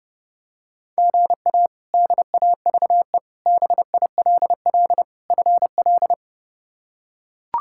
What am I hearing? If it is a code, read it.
Morse code, heard as a keyed tone, about 28 wpm, 700 Hz. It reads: GA DAVE BILL FL
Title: K1USN SST QSO Practice - 30wpm